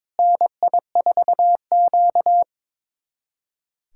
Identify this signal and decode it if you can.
Morse code, heard as a keyed tone, about 22 wpm, 700 Hz. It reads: NI4Q
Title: USA Callsign DE Callsign - 4 Characters - 22wpm